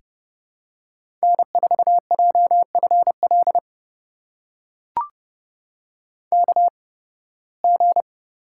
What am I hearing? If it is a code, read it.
Morse code, heard as a keyed tone, about 30 wpm, spaced as for 20 wpm, 700 Hz. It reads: N4JFL KG
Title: USA Call Signs - 5 Characters - 8x Character Spacing - 30wpm